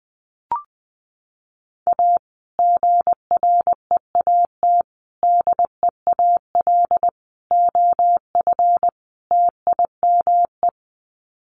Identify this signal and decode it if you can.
Morse code, heard as a keyed tone, about 20 wpm, 700 Hz. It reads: A GREAT DEAL OF TIME
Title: Sets of 5 Words Encore2 20wpm